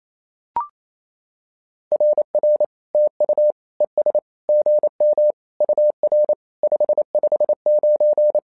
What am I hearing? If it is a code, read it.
Morse code, heard as a keyed tone, about 28 wpm, 600 Hz. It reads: RR TU ES GM UR 559
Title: SOTA QSO Practice 28wpm